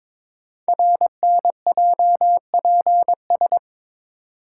Morse code, heard as a keyed tone, about 22 wpm, 700 Hz. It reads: RNJPS
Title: Code Groups - 5 Letters - 22wpm